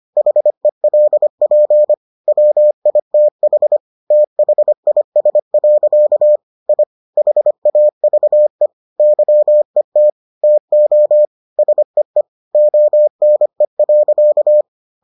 Morse code, heard as a keyed tone, about 25 wpm, 600 Hz. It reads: HELP WITH THIS. I HAVE YET TO SEE ONE.